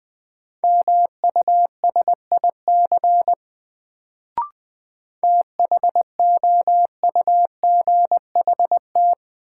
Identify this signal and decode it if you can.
Morse code, heard as a keyed tone, about 20 wpm, 700 Hz. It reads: MUSIC THOUGHT
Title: Sentences from Top 300 Words - New Word Drill - 20wpm